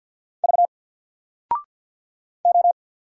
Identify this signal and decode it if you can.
Morse code, heard as a keyed tone, about 50 wpm, 700 Hz. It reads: V O
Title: Single Letter-Number 50wpm